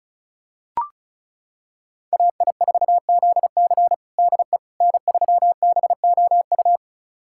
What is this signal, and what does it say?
Morse code, heard as a keyed tone, about 34 wpm, 690 Hz. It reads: AI4ZC DE N3BOU
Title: USA Callsign DE Callsign - 5 Characters - 35wpm